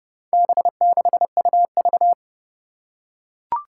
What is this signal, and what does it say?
Morse code, heard as a keyed tone, about 32 wpm, 700 Hz. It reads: B6UV
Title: Four Challenging Letter-Number Combinations 30wpm